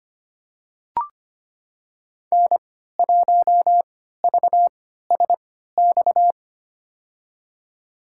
Morse code, heard as a keyed tone, about 25 wpm, spaced as for 14 wpm, 700 Hz. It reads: N1VSX
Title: USA Call Signs - 5 Characters - 3x Character Spacing - 25wpm